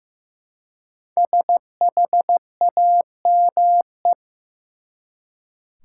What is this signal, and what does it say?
Morse code, heard as a keyed tone, about 15 wpm, 700 Hz. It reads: SHAME
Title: Five Letter Words - 15wpm